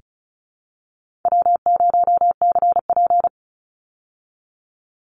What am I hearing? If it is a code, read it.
Morse code, heard as a keyed tone, about 34 wpm, 700 Hz. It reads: W0CP
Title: Easy Call Signs Repeated 3x - 35wpm